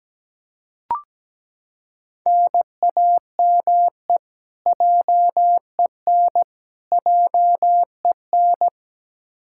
Morse code, heard as a keyed tone, about 17 wpm, 700 Hz. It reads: NAME JEN JEN
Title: Practice QSO - Exchange 1 - Beginning - 17wpm